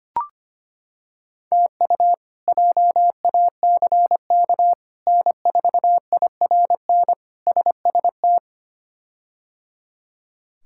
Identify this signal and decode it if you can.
Morse code, heard as a keyed tone, about 25 wpm, 700 Hz. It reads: TU JACK N4IRN SST
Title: K1USN SST QSO Practice - 25wpm